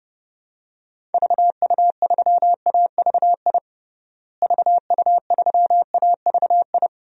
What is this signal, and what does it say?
Morse code, heard as a keyed tone, about 30 wpm, 700 Hz. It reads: VU3AVI VU3AVI
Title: DXCC-100 Call Signs - 6 Characters - Repeated 2x - 30wpm